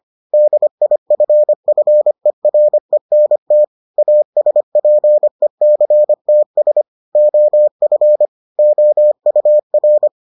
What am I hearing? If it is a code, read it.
Morse code, heard as a keyed tone, about 25 wpm, 600 Hz. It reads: DIFFERENT ASPECTS OF OUR